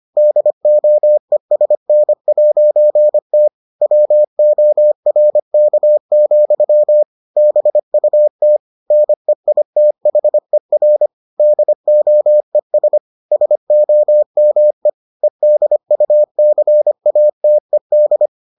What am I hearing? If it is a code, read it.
Morse code, heard as a keyed tone, about 25 wpm, 600 Hz. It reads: DOESN'T WORK, BUT NEITHER DOES SOME EDUCATED